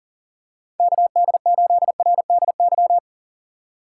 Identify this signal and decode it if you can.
Morse code, heard as a keyed tone, about 40 wpm, 700 Hz. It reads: KD8RDY
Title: USA Call Signs - 6 Characters - 4x Character Spacing - 40wpm